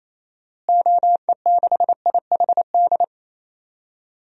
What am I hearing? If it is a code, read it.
Morse code, heard as a keyed tone, about 28 wpm, 700 Hz. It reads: OE6IHD
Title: DXCC-100 Call Signs - 6 Characters - Repeated 2x - 28wpm